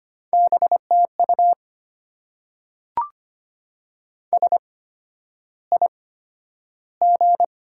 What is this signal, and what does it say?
Morse code, heard as a keyed tone, about 25 wpm, spaced as for 11 wpm, 700 Hz. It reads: BTU SIG
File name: Top 90 QSO Elements - 8x Character Spacing - 25wpm